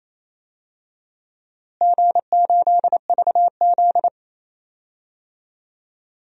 Morse code, heard as a keyed tone, about 28 wpm, 700 Hz. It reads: G8VZ
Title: DXCC-100 Call Signs - 4 Characters - Repeated 3x - 28wpm